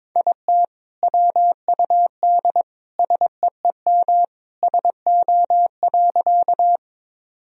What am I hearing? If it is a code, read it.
Morse code, heard as a keyed tone, about 21 wpm, 700 Hz. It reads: IT WUD SEEM SO.
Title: Sentences from Top 300 Words and Abbreviations 22wpm